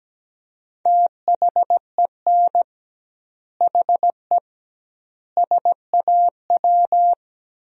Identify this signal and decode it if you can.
Morse code, heard as a keyed tone, about 17 wpm, 700 Hz. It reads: THEN HE SAW
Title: 2x Word-Spacing - Sentences from Top 400 Words - 17wpm